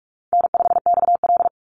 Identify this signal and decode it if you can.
Morse code, heard as a keyed tone, about 44 wpm, 700 Hz. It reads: N5XL
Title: NA SOTA Chaser Callsigns 45wpm